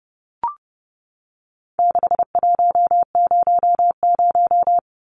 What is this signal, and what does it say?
Morse code, heard as a keyed tone, about 30 wpm, 700 Hz. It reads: 6100
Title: Numbers - 4-Digit 30wpm